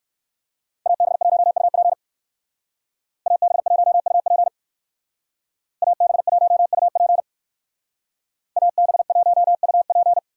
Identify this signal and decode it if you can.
Morse code, heard as a keyed tone, about 53 wpm, 700 Hz. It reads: AB1UP AB1UP AB1UP AB1UP
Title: Speed Racing - USA 2x2 Call Signs 45wpm